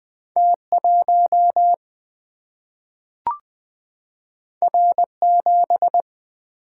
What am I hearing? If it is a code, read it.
Morse code, heard as a keyed tone, about 20 wpm, 700 Hz. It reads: T 1 R 7
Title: Single-Letter Single-Number 20wpm